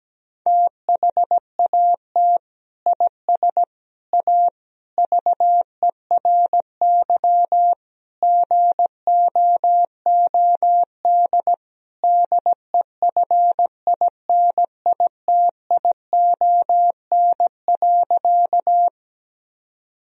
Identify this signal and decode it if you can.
Morse code, heard as a keyed tone, about 17 wpm, 700 Hz. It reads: THAT IS A VERY GOOD DEFINITION.
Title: Sentences from Top 1000 Words - Review Part 4 - 17wpm